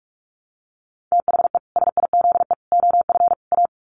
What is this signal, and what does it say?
Morse code, heard as a keyed tone, about 44 wpm, 700 Hz. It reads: THE SIZE OF A
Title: Sets of 4 Words 45wpm